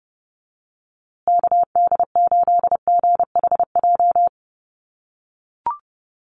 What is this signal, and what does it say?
Morse code, heard as a keyed tone, about 30 wpm, 700 Hz. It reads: KD8GHJ
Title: USA Call Signs Repeated 2x 30wpm